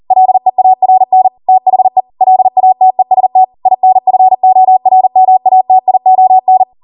Morse code, heard as a keyed tone, about 40 wpm, 750 Hz. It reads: LEARN THE LATEST INFORMATION